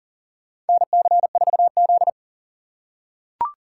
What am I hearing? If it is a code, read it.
Morse code, heard as a keyed tone, about 40 wpm, 700 Hz. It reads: NC4Z
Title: USA Call Signs - 2x Character Spacing - 40wpm